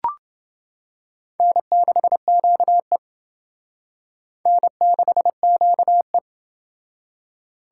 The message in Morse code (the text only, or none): N6QE N6QE